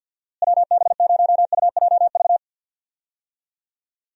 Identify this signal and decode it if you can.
Morse code, heard as a keyed tone, about 51 wpm, 700 Hz. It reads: WB0UJV
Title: USA Call Signs - 6 Characters - Repeated 2x - 50wpm